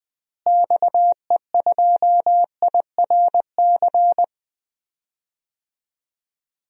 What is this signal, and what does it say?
Morse code, heard as a keyed tone, about 20 wpm, 700 Hz. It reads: XE2IRC
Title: DXCC-100 Call Signs - 6 Characters - Repeated 3x - 20wpm